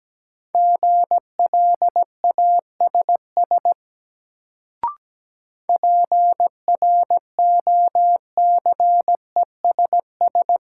Morse code, heard as a keyed tone, about 17 wpm, 700 Hz. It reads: GLASS PROCESS
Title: Sentences from Top 1000 Words - Review Part 3 - New Word Drill - 17wpm